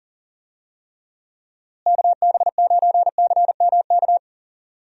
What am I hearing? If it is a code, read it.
Morse code, heard as a keyed tone, about 40 wpm, 700 Hz. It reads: KB9CMK
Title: USA Call Signs - 6 Characters - Repeated 2x - 40wpm